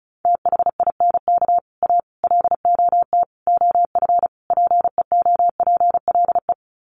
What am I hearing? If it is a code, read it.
Morse code, heard as a keyed tone, about 34 wpm, 700 Hz. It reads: THINK A LOT OF PEOPLE
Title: Sets of 5 Words 35wpm